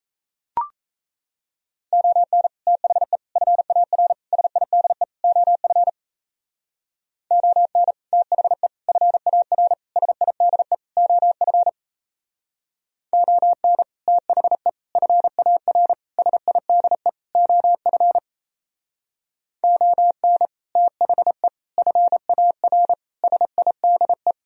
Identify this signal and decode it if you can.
Morse code, heard as a keyed tone, about 34 wpm, 700 Hz. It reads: ON THE FAR SIDE OF ON THE FAR SIDE OF ON THE FAR SIDE OF ON THE FAR SIDE
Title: Speed-Racing - Sets of 5 Words 28wpm